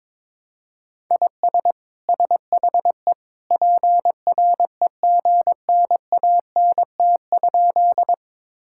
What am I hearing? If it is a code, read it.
Morse code, heard as a keyed tone, about 22 wpm, 700 Hz. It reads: IS SHE PREGNANT?
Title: Sentences from Top 1000 Words - Review Part 6 - 22wpm